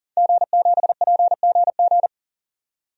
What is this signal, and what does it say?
Morse code, heard as a keyed tone, about 40 wpm, 700 Hz. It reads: G7PGG
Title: DXCC-100 Call Signs - 5 Characters - 40wpm